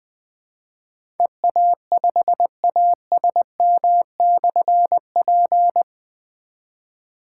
Morse code, heard as a keyed tone, about 20 wpm, 700 Hz. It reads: EA5ASM/P
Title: Difficult Call Signs 20wpm